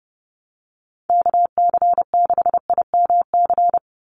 Morse code, heard as a keyed tone, about 30 wpm, 700 Hz. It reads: KC6IMC
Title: USA Call Signs 30wpm